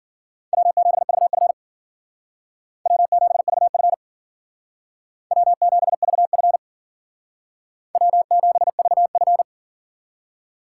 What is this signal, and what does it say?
Morse code, heard as a keyed tone, about 51 wpm, 700 Hz. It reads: W7VF W7VF W7VF W7VF